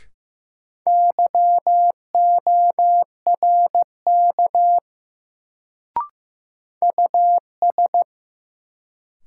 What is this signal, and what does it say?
Morse code, heard as a keyed tone, about 15 wpm, 700 Hz. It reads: YORK US
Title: Sets of 2 Words - New Word Drill - 15wpm